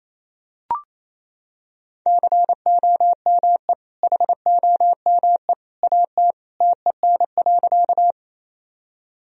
Morse code, heard as a keyed tone, about 28 wpm, 700 Hz. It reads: COME HOME AT TEN.
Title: Sentences from Top 1000 Words - Review Part 2 - 28wpm